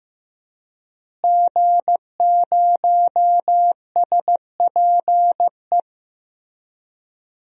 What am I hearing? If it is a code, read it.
Morse code, heard as a keyed tone, about 16 wpm, 700 Hz. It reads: G0SPE
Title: DXCC-100 Call Signs - 5 Characters - Repeated 3x - 15wpm